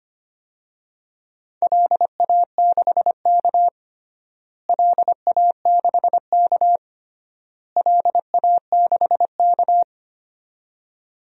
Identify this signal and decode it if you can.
Morse code, heard as a keyed tone, about 25 wpm, 700 Hz. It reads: LA6K LA6K LA6K
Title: DXCC-100 Call Signs - 4 Characters - Repeated 3x - 25wpm